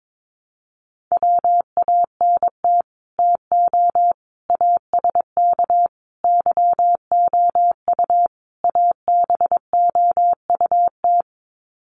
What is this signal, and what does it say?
Morse code, heard as a keyed tone, about 22 wpm, 700 Hz. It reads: WANT TO ASK YOU ABOUT
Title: Sets of 5 Words Encore2 22wpm